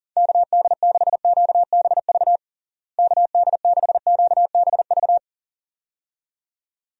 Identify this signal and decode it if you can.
Morse code, heard as a keyed tone, about 40 wpm, 700 Hz. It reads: KD6QBV KD6QBV
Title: USA Call Signs - 6 Characters - Repeated 2x - 40wpm